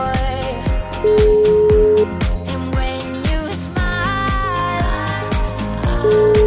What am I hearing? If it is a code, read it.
Ringback tone (Europe, 425 Hz), with noise.